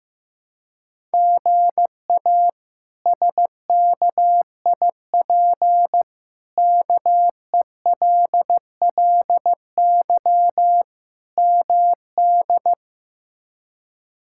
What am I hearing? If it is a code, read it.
Morse code, heard as a keyed tone, about 15 wpm, 700 Hz. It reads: GA SKIP KELLY MD
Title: K1USN SST QSO Practice - 15wpm